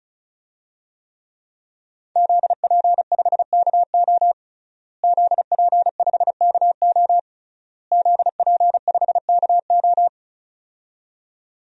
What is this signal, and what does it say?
Morse code, heard as a keyed tone, about 36 wpm, 700 Hz. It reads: ZP5KO ZP5KO ZP5KO
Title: DXCC-100 Call Signs - 5 Characters - Repeated 3x - 35wpm